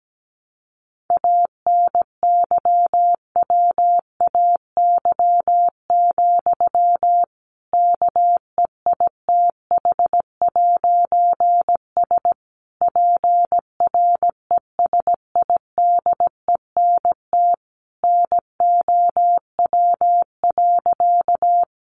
Morse code, heard as a keyed tone, about 17 wpm, 700 Hz. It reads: ANYWAY, KEITH'S PRESIDENT NOW.
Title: A Sound Of Thunder 17wpm